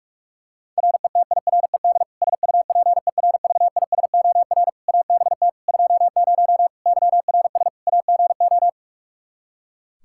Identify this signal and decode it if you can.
Morse code, heard as a keyed tone, about 41 wpm, 700 Hz. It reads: RETIRED SUPERVISOR ABT 20 YRS AGO
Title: Practice QSO - Exchange 3 - More Info - 45wpm